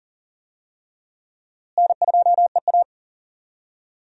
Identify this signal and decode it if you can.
Morse code, heard as a keyed tone, about 40 wpm, 700 Hz. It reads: N2EA
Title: USA Call Signs - 4 Characters - Repeated 3x - 40wpm